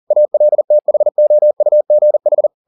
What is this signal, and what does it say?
Morse code, heard as a keyed tone, about 40 wpm, 600 Hz. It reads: ALTHOUGH
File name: the old man and the sea 2 40wpm